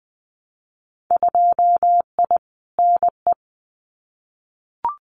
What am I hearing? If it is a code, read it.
Morse code, heard as a keyed tone, about 20 wpm, 700 Hz. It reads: 2I NE
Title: Winter Field Day Exchanges Encore 20wpm